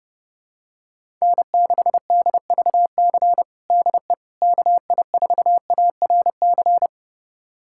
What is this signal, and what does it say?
Morse code, heard as a keyed tone, about 30 wpm, 700 Hz. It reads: N6DVC DE KI4ARC
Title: USA Callsign DE Callsign 30wpm